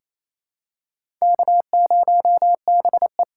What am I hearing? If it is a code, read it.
Morse code, heard as a keyed tone, about 28 wpm, 690 Hz. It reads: K0BE
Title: USA Callsign DE Callsign - 4 Characters - 28wpm